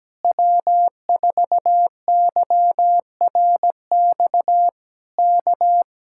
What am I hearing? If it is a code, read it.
Morse code, heard as a keyed tone, about 17 wpm, 700 Hz. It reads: W4YRX K
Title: Speed-Racing Common QSO Phrases Encore - 17wpm